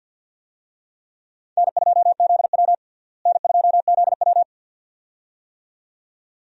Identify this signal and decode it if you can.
Morse code, heard as a keyed tone, about 52 wpm, 700 Hz. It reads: N2ZW N2ZW
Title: USA Call Signs - 4 Characters - Repeated 2x - 50wpm